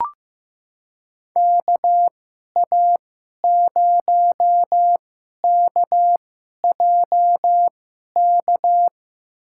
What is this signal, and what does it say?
Morse code, heard as a keyed tone, about 15 wpm, spaced as for 11 wpm, 700 Hz. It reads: KA0KJK
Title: USA Call Signs - 6 Characters - 2x Character Spacing - 15wpm